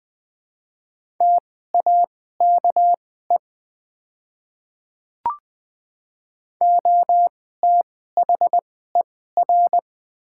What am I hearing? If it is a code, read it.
Morse code, heard as a keyed tone, about 20 wpm, spaced as for 14 wpm, 700 Hz. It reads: TAKE OTHER
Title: 2x Character Spacing - Top 200 Words - 20wpm